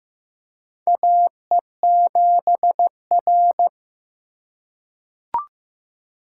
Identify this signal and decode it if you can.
Morse code, heard as a keyed tone, about 15 wpm, 700 Hz. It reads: A E 7 R
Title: USA Call Signs - 4 Characters - 6x Character Spacing - 15wpm